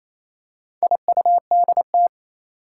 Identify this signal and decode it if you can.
Morse code, heard as a keyed tone, about 28 wpm, 700 Hz. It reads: IUDT
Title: Four Challenging Letter-Number Combinations 28wpm